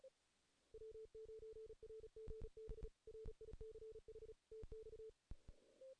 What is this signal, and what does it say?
Morse code, heard as a keyed tone, about 35 wpm, 440 Hz. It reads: W9RGB RICH TX
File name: 2016 JAN CW NAQP / 2016 Jan naqp k5ot - 0-2